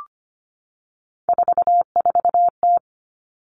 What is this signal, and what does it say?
Morse code, heard as a keyed tone, about 25 wpm, 700 Hz. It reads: 44T
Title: Three Challenging Letter-Number Combinations 25wpm